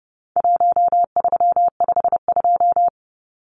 Morse code, heard as a keyed tone, about 30 wpm, 700 Hz. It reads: 1352
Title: Numbers - Contest Style - 4-Digit 30wpm